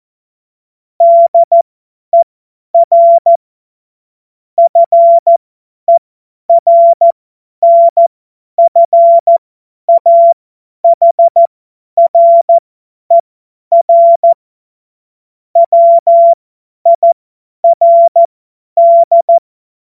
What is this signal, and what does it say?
Morse code, heard as a keyed tone, about 14 wpm, spaced as for 10 wpm, 680 Hz. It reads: DER FERNFAHRER WIRD
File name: Morsekurs / Witze / Witz 07.November 2023 50 F70